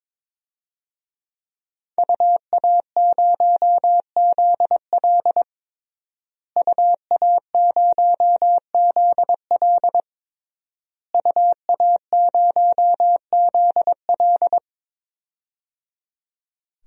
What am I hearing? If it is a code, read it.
Morse code, heard as a keyed tone, about 22 wpm, 700 Hz. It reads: UA0ZL UA0ZL UA0ZL
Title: DXCC-100 Call Signs - 5 Characters - Repeated 3x - 22wpm